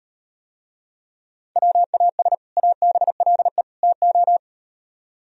Morse code, heard as a keyed tone, about 37 wpm, 700 Hz. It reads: WAS ABLE TO